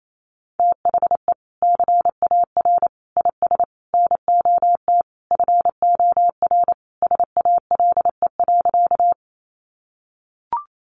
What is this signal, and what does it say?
Morse code, heard as a keyed tone, about 28 wpm, 700 Hz. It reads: THE CAR IS NOT FOR SALE.
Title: Sentences from Top 1400 Words 28wpm